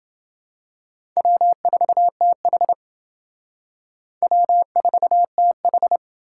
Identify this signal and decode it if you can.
Morse code, heard as a keyed tone, about 28 wpm, 700 Hz. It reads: W4TH W4TH